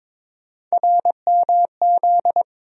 Morse code, heard as a keyed tone, about 21 wpm, 700 Hz. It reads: RMZ
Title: USA Call Sign Postfixes - 3 Letters 22wpm